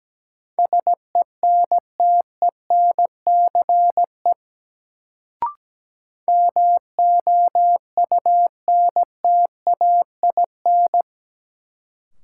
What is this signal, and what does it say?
Morse code, heard as a keyed tone, about 17 wpm, 700 Hz. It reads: SENTENCE MOUNTAIN
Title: Sentences from Top 1000 Words - Review Part 1 - New Word Drill - 17wpm